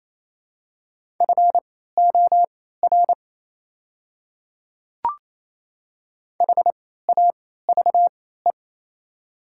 Morse code, heard as a keyed tone, about 28 wpm, spaced as for 16 wpm, 700 Hz. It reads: FOR HAVE
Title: 3x Character Spacing - Top 200 Words - 28wpm